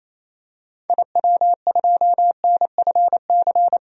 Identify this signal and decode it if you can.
Morse code, heard as a keyed tone, about 28 wpm, 700 Hz. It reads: IW2NFC
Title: DXCC-100 Call Signs - 6 Characters - 28wpm